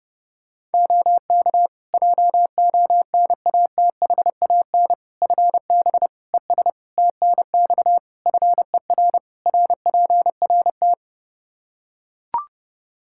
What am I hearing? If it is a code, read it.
Morse code, heard as a keyed tone, about 32 wpm, 700 Hz. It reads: OK JONATHAN FB ES TNX FER RPRT
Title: Practice QSO - Exchange 2 - Info - 30wpm